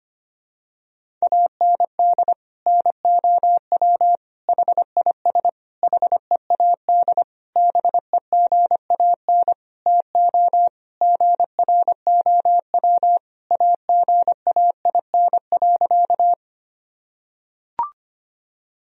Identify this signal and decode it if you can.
Morse code, heard as a keyed tone, about 25 wpm, 700 Hz. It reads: AND NOW HIS HEAD BEGAN TO GROW AGAIN.
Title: Sentences from Top 1000 Words - Review Part 1 - 25wpm